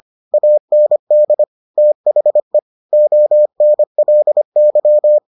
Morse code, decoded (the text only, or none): AND THE ONLY